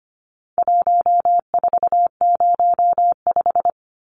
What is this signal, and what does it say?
Morse code, heard as a keyed tone, about 25 wpm, 700 Hz. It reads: 1405
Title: Numbers - 4-Digit 25wpm